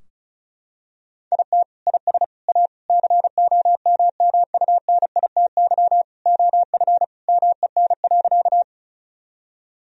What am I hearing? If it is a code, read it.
Morse code, heard as a keyed tone, about 36 wpm, 700 Hz. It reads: IT IS A COMMUNITY OF MEN.
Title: Sentences from Top 1000 Words - Review Part 4 - 35wpm